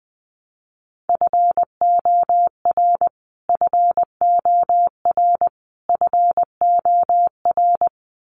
Morse code, heard as a keyed tone, about 20 wpm, 700 Hz. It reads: FOR FOR FOR
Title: Top 300 Words Repeated 20wpm